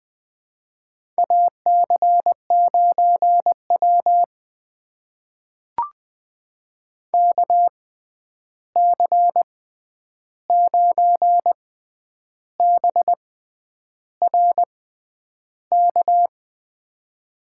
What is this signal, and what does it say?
Morse code, heard as a keyed tone, about 20 wpm, spaced as for 9 wpm, 700 Hz. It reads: AC9W KC9BRK
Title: USA Call Signs - 6x Character Spacing - 20wpm